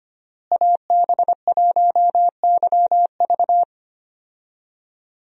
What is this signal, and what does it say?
Morse code, heard as a keyed tone, about 25 wpm, 700 Hz. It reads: AB1YV
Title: USA Call Signs - 5 Characters - Repeated 2x - 25wpm